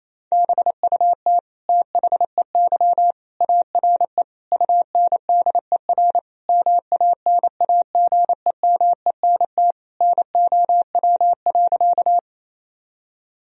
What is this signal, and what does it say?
Morse code, heard as a keyed tone, about 27 wpm, 700 Hz. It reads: BUT THEY ARE UNDER MANAGEMENT NOW.
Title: Sentences from Top 1000 Words - Review Part 4 - 28wpm